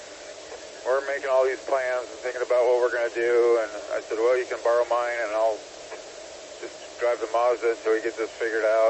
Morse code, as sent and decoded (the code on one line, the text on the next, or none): none